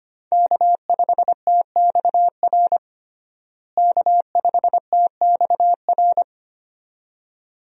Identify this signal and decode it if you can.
Morse code, heard as a keyed tone, about 25 wpm, 700 Hz. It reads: K5TXR K5TXR
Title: USA Call Signs - 5 Characters - Repeated 2x - 25wpm